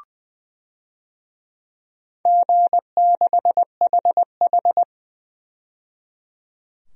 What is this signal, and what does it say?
Morse code, heard as a keyed tone, about 20 wpm, 700 Hz. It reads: G6HH
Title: DXCC-100 Call Signs - 4 Characters - 20wpm